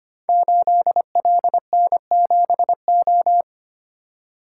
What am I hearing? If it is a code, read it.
Morse code, heard as a keyed tone, about 25 wpm, 700 Hz. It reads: 8LN7O
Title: Code Groups - 5 Letter:Numbers - 25wpm